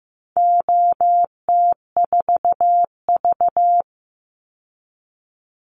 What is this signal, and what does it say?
Morse code, heard as a keyed tone, about 15 wpm, 700 Hz. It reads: OT4V
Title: Call Signs Repeated 2x - 15wpm